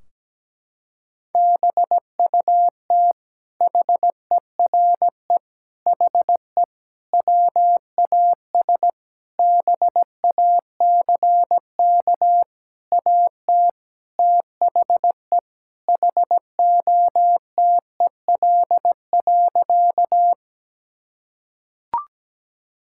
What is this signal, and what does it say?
Morse code, heard as a keyed tone, about 17 wpm, 700 Hz. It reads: BUT HERE HE WAS BACK AT THE HOTEL.
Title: Sentences from Top 1600 Words 17wpm